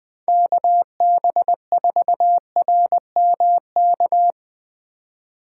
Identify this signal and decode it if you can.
Morse code, heard as a keyed tone, about 21 wpm, 700 Hz. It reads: KB4RMK